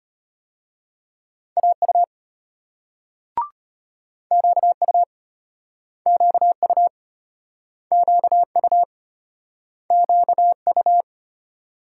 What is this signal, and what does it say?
Morse code, heard as a keyed tone, about 34 wpm, 700 Hz. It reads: AU QU QU QU QU